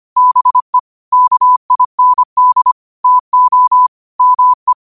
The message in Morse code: BE KIND TO ME

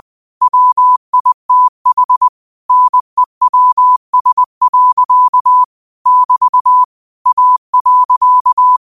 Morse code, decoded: WITH NEWS. <BT> A.

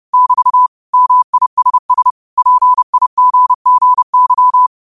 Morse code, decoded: <BT> MISS PIGGY

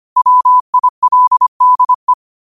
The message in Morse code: WILDE

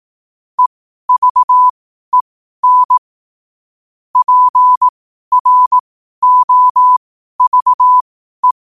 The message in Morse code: EVEN PROVE

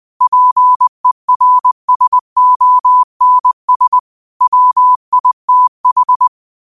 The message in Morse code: PERSONS WITH

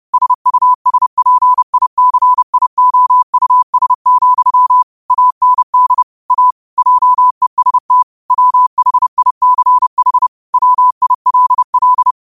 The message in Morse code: SUSPICIOUS, AND A JEST WHICH WILL

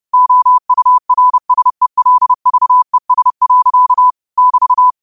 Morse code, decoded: OURSELVES. <BT>

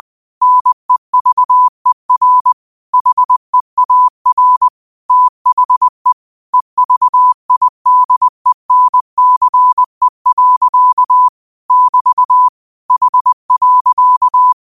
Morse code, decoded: NEVER HEAR THE EVIDENCE. <BT> H.

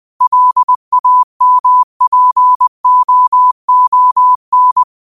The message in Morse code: LAMPOON

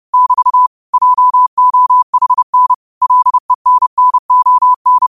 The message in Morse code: <BT> JOHN LENNON

